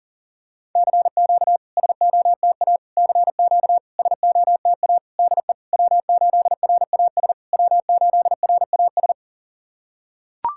Morse code, heard as a keyed tone, about 40 wpm, 700 Hz. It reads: CQ SOTA CQ SOTA DE W8RAS W8RAS
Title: SOTA QSO Practice 40wpm